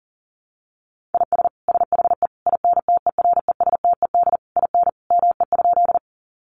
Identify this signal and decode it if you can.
Morse code, heard as a keyed tone, about 40 wpm, 700 Hz. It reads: IS SHE INTERESTED IN ME?
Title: Sentences from Top 900 Words 40wpm